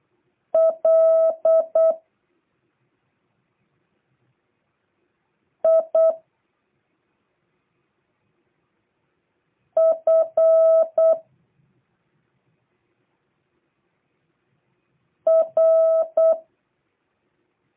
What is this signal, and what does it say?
Morse code, heard as a keyed tone, about 8 wpm, 650 Hz. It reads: L I F R